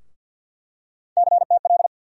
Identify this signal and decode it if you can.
Morse code, heard as a keyed tone, about 47 wpm, 700 Hz. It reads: CTL
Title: USA Call Sign Postfixes - 3 Letters 50wpm